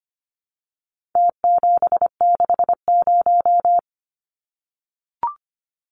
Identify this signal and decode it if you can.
Morse code, heard as a keyed tone, about 25 wpm, 700 Hz. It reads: T760
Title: Numbers - Contest Style - 4-Digit 25wpm